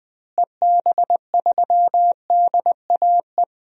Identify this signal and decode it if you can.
Morse code, heard as a keyed tone, about 20 wpm, 700 Hz. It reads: EB3DAE
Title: DXCC-100 Call Signs - 6 Characters - Repeated 3x - 20wpm